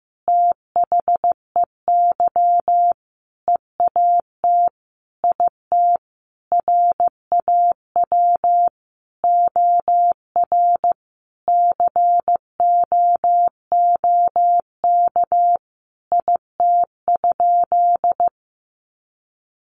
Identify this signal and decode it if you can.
Morse code, heard as a keyed tone, about 14 wpm, 700 Hz. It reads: THEY EAT IT RAW OR COOK IT?
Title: Sentences from Top 1500 Words 15wpm